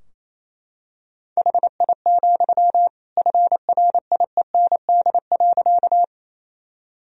Morse code, heard as a keyed tone, about 28 wpm, 700 Hz. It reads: HI, FRIEND.
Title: Sentences from Top 1000 Words - Review Part 3 - 28wpm